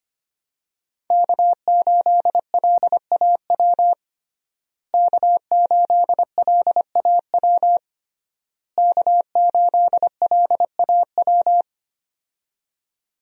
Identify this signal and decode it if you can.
Morse code, heard as a keyed tone, about 25 wpm, 700 Hz. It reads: K8LAW K8LAW K8LAW
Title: USA Call Signs - 5 Characters - Repeated 3x - 25wpm